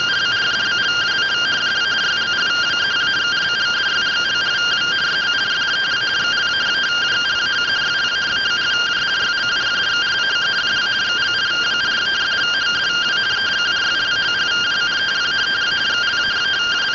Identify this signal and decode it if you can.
Two-tone teleprinter signal, about 170 Hz shift.